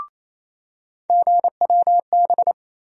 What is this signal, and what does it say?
Morse code, heard as a keyed tone, about 28 wpm, 700 Hz. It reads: GWB
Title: USA Call Sign Postfixes - 3 Letters 28wpm